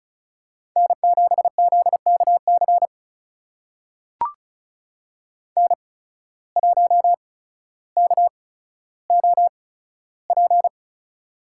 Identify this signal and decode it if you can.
Morse code, heard as a keyed tone, about 36 wpm, spaced as for 15 wpm, 700 Hz. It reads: N7ZKC N1KOP
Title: USA Call Signs - 5 Characters - 8x Character Spacing - 35wpm